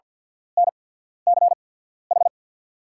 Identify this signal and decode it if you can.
Morse code, heard as a keyed tone, about 48 wpm, spaced as for 14 wpm, 690 Hz. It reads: NCH
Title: 8x Character Spacing - Top 500 Words - 50wpm